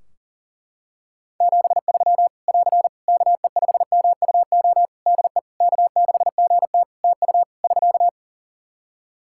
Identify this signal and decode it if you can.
Morse code, heard as a keyed tone, about 40 wpm, 700 Hz. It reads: 73 <AR> KE5MUO DE K6GT TU <SK>
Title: Practice QSO - Exchange 4 - Ending - 40wpm